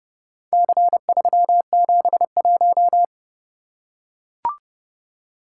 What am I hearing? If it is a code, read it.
Morse code, heard as a keyed tone, about 26 wpm, 700 Hz. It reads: C371